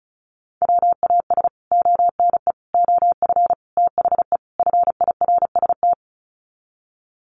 Morse code, heard as a keyed tone, about 34 wpm, 700 Hz. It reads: WAS ONE OF THE FIRST
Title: Sets of 5 Words 35wpm